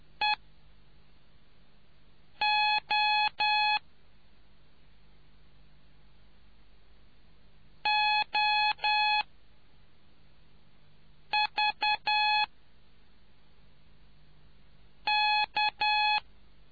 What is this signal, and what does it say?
Morse code, heard as a keyed tone, about 10 wpm, spaced as for 4 wpm, 840 Hz. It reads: EO OV K